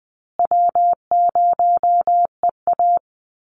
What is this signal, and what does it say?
Morse code, heard as a keyed tone, about 20 wpm, 700 Hz. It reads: W0EA
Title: Call Signs Repeated 2x - 20wpm